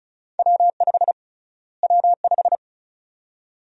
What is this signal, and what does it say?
Morse code, heard as a keyed tone, about 36 wpm, 700 Hz. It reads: W5 W5
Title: USA Call Sign Prefixes - Repeated 2x - 35wpm